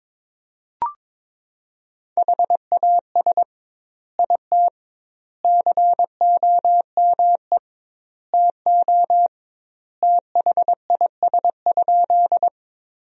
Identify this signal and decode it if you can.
Morse code, heard as a keyed tone, about 22 wpm, 700 Hz. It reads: HAS IT COME TO THIS?
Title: 2x Word-Spacing - Sentences from Top 100 Words - 22wpm